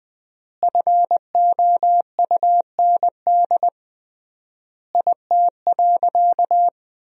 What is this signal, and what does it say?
Morse code, heard as a keyed tone, about 20 wpm, 700 Hz. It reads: FOUND IT.
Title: 3x Word-Spacing - Sentences from Top 300 Words - 20wpm